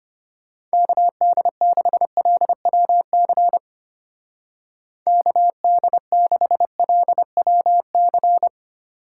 Morse code, heard as a keyed tone, about 31 wpm, 700 Hz. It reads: KD6LWC KD6LWC
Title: Speed Racing - USA 2x3 Call Signs 25wpm